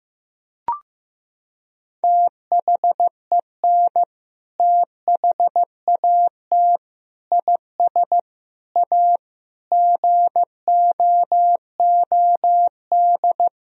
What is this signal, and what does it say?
Morse code, heard as a keyed tone, about 15 wpm, 700 Hz. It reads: THEN THAT IS A GOOD
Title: Sentences from Top 1000 Words - Review Part 4 - 15wpm